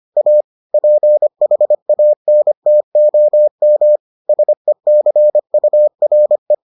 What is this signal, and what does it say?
Morse code, heard as a keyed tone, about 25 wpm, 600 Hz. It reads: A PHANTOM SECURE